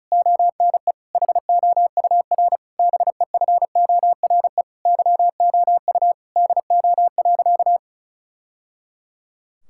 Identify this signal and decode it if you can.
Morse code, heard as a keyed tone, about 36 wpm, 700 Hz. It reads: ONE HOUR BEFORE YOU DO.
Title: Sentences from Top 1000 Words - Review Part 6 - 35wpm